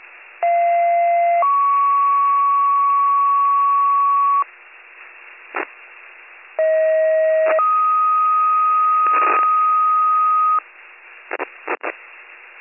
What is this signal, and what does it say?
Two-tone page, heard as a one-second tone, then a three-second tone, about 690 then 1120 Hz, about 640 then 1250 Hz. It is heard from 0.4 to 10.6 s.